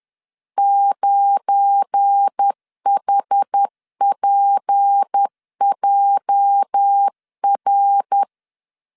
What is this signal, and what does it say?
Morse code, heard as a keyed tone, about 10 wpm, 800 Hz. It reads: 9HPJR
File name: LEZIONI / 10 SIGNS / 10C DOT 10wpm 2940